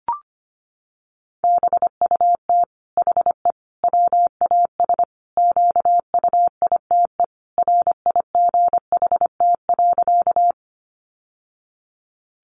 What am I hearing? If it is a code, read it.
Morse code, heard as a keyed tone, about 25 wpm, 700 Hz. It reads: BUT HE WAS QUITE RIGHT.
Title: Sentences from Top 700 Words 25wpm